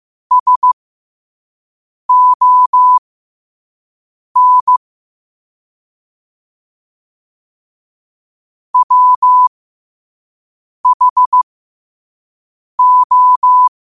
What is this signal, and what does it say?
Morse code, heard as a keyed tone, about 15 wpm, spaced as for 5 wpm, 1000 Hz. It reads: SON WHO
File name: qotdfiles / 5wpm